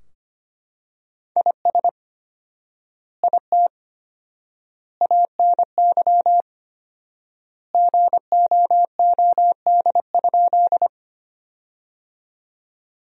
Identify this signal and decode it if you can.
Morse code, heard as a keyed tone, about 25 wpm, 700 Hz. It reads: IS IT ANY GOOD?
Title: 4x Word-Spacing - Sentences from Top 200 Words - 25wpm